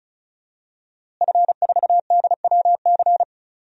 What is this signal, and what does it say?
Morse code, heard as a keyed tone, about 36 wpm, 700 Hz. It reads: F4DWC
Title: DXCC-100 Call Signs - 5 Characters - Repeated 2x - 35wpm